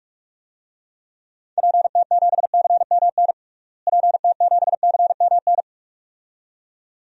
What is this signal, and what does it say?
Morse code, heard as a keyed tone, about 46 wpm, 700 Hz. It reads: PT7CMN PT7CMN
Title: DXCC-100 Call Signs - 6 Characters - Repeated 2x - 45wpm